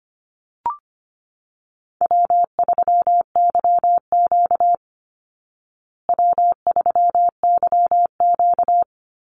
Morse code, heard as a keyed tone, about 25 wpm, 700 Hz. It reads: W3YQ W3YQ
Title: Call Signs Repeated 2x - 25wpm